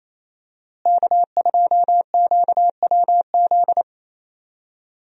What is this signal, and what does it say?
Morse code, heard as a keyed tone, about 28 wpm, 700 Hz. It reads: K2QWZ
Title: USA Call Signs - 5 Characters - Repeated 2x - 28wpm